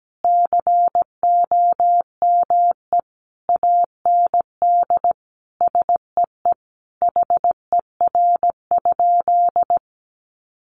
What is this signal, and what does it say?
Morse code, heard as a keyed tone, about 17 wpm, 700 Hz. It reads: COME AND SEE HER?
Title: Sentences from Top 100 Words 17wpm